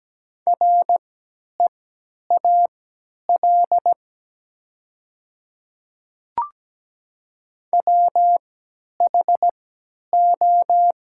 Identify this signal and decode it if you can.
Morse code, heard as a keyed tone, about 17 wpm, spaced as for 10 wpm, 700 Hz. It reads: REAL WHO
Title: 3x Character Spacing - Top 300 Words - 17wpm